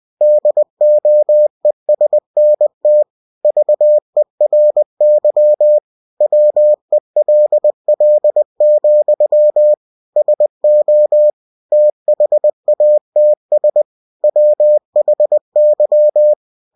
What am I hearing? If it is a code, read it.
Morse code, heard as a keyed tone, about 20 wpm, 600 Hz. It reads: DOESNT VERY WELL, SO THATS WHY
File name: cw-books / winnie-the-pooh / winnie-20wpm-page-0011